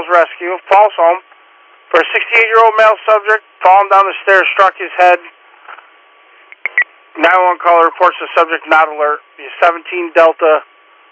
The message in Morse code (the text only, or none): none